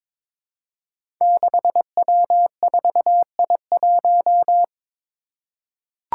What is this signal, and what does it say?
Morse code, heard as a keyed tone, about 22 wpm, 700 Hz. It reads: 6W4I1
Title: Five Challenging Letter-Number Combinations 22wpm